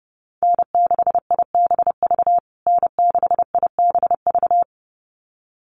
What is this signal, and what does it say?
Morse code, heard as a keyed tone, about 30 wpm, 700 Hz. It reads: N6IBV N6IBV
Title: USA Call Signs Repeated 2x 30wpm